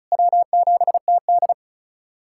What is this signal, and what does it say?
Morse code, heard as a keyed tone, about 34 wpm, 700 Hz. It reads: W7TD
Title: USA Callsign DE Callsign - 4 Characters - 35wpm